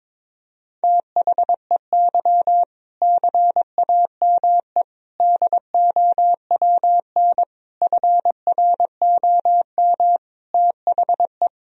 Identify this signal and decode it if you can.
Morse code, heard as a keyed tone, about 22 wpm, 700 Hz. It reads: THEY CAME DOWN FROM THE
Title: Sentences from Top 1000 Words - Review Part 1 - 22wpm